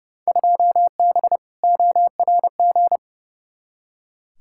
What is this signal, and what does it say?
Morse code, heard as a keyed tone, about 30 wpm, 700 Hz. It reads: 2B ORG
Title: Field Day Exchanges Encore - 30wpm